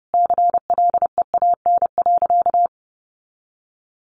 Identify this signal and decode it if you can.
Morse code, heard as a keyed tone, about 30 wpm, 700 Hz. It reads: CLEAN.
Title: Easy Phrases 30wpm